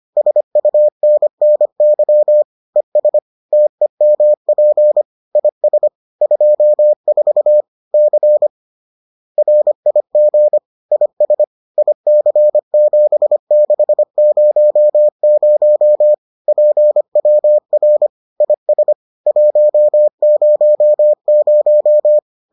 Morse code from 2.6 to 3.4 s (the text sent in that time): ES